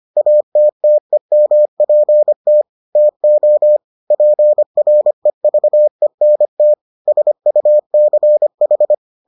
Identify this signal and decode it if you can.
Morse code, heard as a keyed tone, about 25 wpm, 600 Hz. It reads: ATTEMPT TO PREVENT SUCH